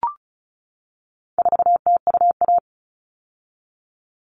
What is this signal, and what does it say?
Morse code, heard as a keyed tone, about 34 wpm, 700 Hz. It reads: 4TUA
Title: All Cut Numbers - 4-Digit 35wpm